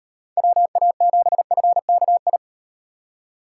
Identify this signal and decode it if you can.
Morse code, heard as a keyed tone, about 36 wpm, 700 Hz. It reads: WA7FKI